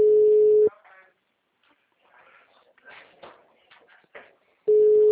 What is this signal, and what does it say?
Ringback tone (Europe, 425 Hz).